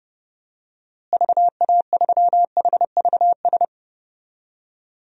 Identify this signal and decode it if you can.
Morse code, heard as a keyed tone, about 30 wpm, 700 Hz. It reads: VA3HVS
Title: DXCC-100 Call Signs - 6 Characters - Repeated 2x - 30wpm